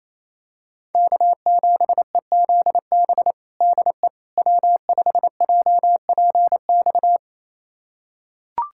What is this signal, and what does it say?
Morse code, heard as a keyed tone, about 28 wpm, 700 Hz. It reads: K7EZB DE W5JPX
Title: USA Callsign DE Callsign - 5 Characters - 28wpm